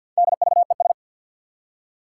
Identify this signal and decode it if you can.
Morse code, heard as a keyed tone, about 51 wpm, 700 Hz. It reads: DVES
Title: Four Challenging Letter-Number Combinations 50wpm